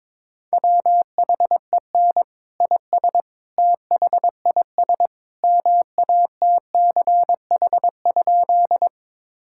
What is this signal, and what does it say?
Morse code, heard as a keyed tone, about 22 wpm, 700 Hz. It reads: WHEN IS THIS MATCH?
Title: Sentences from Top 1600 Words 22wpm